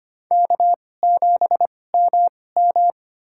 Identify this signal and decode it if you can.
Morse code, heard as a keyed tone, about 25 wpm, spaced as for 18 wpm, 700 Hz. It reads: K7MM
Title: USA Call Signs - 4 Characters - 2x Character Spacing - 25wpm